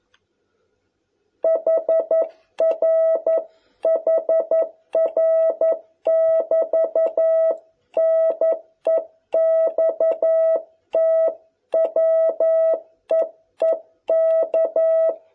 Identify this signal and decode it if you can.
Morse code, heard as a keyed tone, about 11 wpm, 660 Hz. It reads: HR HR <BT> NEXT WEEK